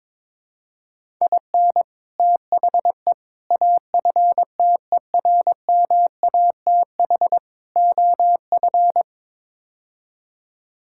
Morse code, heard as a keyed tone, about 22 wpm, 700 Hz. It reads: IN THE AFTERMATH OF
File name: practice / 22wpm / Build A Phrase - 5-Word Phrases - 22wpm